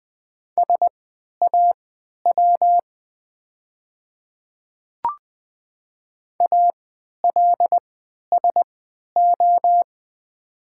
Morse code, heard as a keyed tone, about 20 wpm, spaced as for 11 wpm, 700 Hz. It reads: SAW ALSO
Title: 3x Character Spacing - Top 300 Words - 20wpm